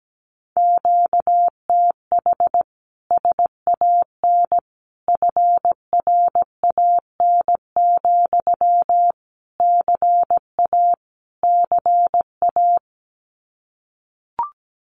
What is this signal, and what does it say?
Morse code, heard as a keyed tone, about 17 wpm, 700 Hz. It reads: QTH SAN FRAN, CA CA
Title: Common QSO Phrases Encore 17wpm